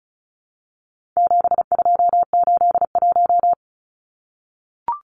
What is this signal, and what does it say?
Morse code, heard as a keyed tone, about 34 wpm, 700 Hz. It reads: 7281
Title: Numbers - 4-Digit 35wpm